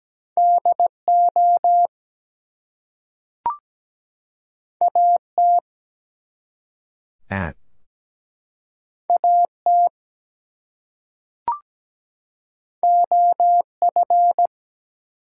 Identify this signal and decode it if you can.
Morse code, heard as a keyed tone, about 17 wpm, 700 Hz. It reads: DO AT AT OF
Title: Two Letter Words - 17wpm